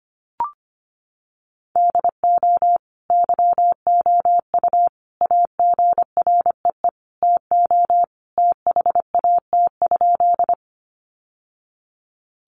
Morse code, heard as a keyed tone, about 25 wpm, 700 Hz. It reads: DO YOU AGREE TO THAT?
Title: Sentences from Top 1100 Words 25wpm